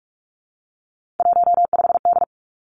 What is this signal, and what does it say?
Morse code, heard as a keyed tone, about 46 wpm, 700 Hz. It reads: 15D
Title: Three Challenging Letter-Number Combinations 45wpm